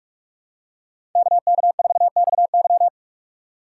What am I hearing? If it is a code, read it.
Morse code, heard as a keyed tone, about 44 wpm, 700 Hz. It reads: KK4XY
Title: USA Call Signs - 3x Character Spacing - 45wpm